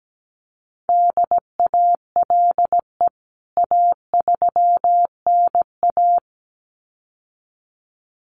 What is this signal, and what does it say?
Morse code, heard as a keyed tone, about 17 wpm, 700 Hz. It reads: DALE A3NA
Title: CWT Exchange 17wpm